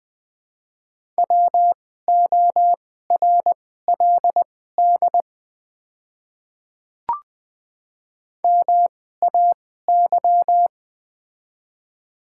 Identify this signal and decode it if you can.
Morse code, heard as a keyed tone, about 20 wpm, spaced as for 14 wpm, 700 Hz. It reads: WORLD MAY
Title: 2x Character Spacing - Top 300 Words - 20wpm